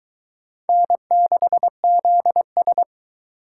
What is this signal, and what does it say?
Morse code, heard as a keyed tone, about 22 wpm, 700 Hz. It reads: N6ZS